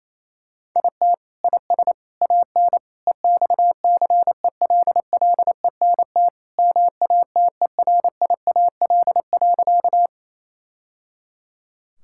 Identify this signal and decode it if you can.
Morse code, heard as a keyed tone, about 28 wpm, 700 Hz. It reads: IT IS AN EXCELLENT MATERIAL.
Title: Sentences from Top 1000 Words - Review Part 5 - 28wpm